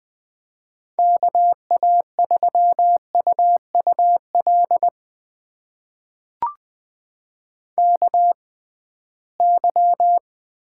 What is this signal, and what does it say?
Morse code, heard as a keyed tone, about 20 wpm, spaced as for 15 wpm, 700 Hz. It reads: KA3UUL KY
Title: USA Call Signs - 6x Character Spacing - 20wpm